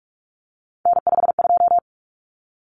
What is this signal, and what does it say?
Morse code, heard as a keyed tone, about 44 wpm, 700 Hz. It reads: N52
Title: Common Cut Numbers - 3-Digit 45wpm